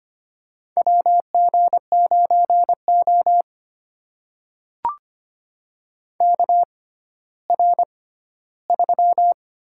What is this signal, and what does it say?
Morse code, heard as a keyed tone, about 25 wpm, spaced as for 14 wpm, 700 Hz. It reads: WG9O KR3
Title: USA Call Signs - 6x Character Spacing - 25wpm